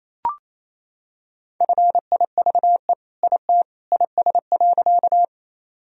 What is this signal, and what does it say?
Morse code, heard as a keyed tone, about 27 wpm, 700 Hz. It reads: FIVE IT IS.
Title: Sentences from Top 1000 Words - Review Part 2 - 28wpm